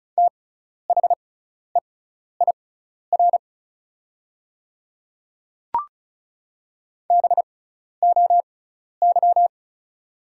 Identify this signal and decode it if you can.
Morse code, heard as a keyed tone, about 36 wpm, spaced as for 12 wpm, 700 Hz. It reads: THEIR BOY
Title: 6x Character Spacing - Top 300 Words - 35wpm